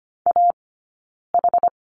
Morse code, heard as a keyed tone, about 25 wpm, 700 Hz. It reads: A H
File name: Lesson 15d - Number 5 - Two Characters 25wpm